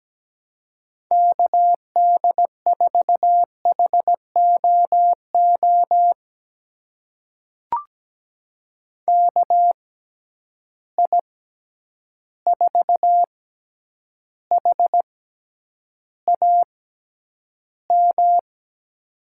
Morse code, heard as a keyed tone, about 17 wpm, spaced as for 9 wpm, 700 Hz. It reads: KD4HOO KI4HAM
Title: USA Call Signs - 6 Characters - 6x Character Spacing - 17wpm